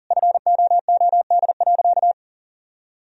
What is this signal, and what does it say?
Morse code, heard as a keyed tone, about 40 wpm, 700 Hz. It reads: FOOD.
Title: Sentences from Top 1000 Words - Review Part 4 - 40wpm